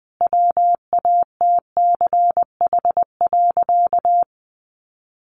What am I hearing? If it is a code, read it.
Morse code, heard as a keyed tone, about 20 wpm, 700 Hz. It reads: WATCH.
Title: A Sound Of Thunder 20wpm